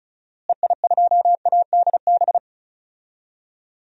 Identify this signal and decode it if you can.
Morse code, heard as a keyed tone, about 35 wpm, 700 Hz. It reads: EI2ADB
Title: DXCC-100 Call Signs - 6 Characters - Repeated 3x - 35wpm